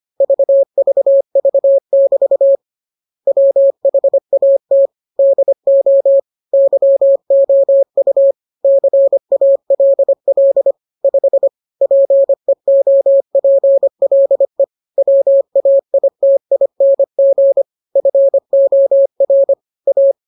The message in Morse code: VVV<BT> WHAT DO YOU CALL 5 PEOPLE WAITING FOR A